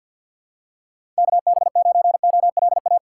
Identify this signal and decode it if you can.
Morse code, heard as a keyed tone, about 53 wpm, 700 Hz. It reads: KB9OLA
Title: USA Call Signs - 3x Character Spacing - 50wpm